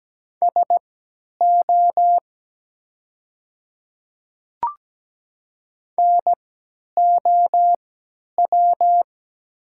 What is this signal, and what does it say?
Morse code, heard as a keyed tone, about 17 wpm, spaced as for 10 wpm, 700 Hz. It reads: SO NOW
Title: 3x Character Spacing - Top 100 Words - 17wpm